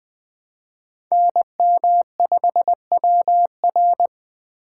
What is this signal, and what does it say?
Morse code, heard as a keyed tone, about 20 wpm, 700 Hz. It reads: NM5WR
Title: Speed Racing - USA Call Signs 17wpm